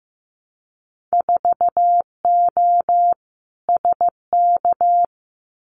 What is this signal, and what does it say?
Morse code, heard as a keyed tone, about 15 wpm, 700 Hz. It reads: 4O SK
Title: Winter Field Day Exchanges Encore 15wpm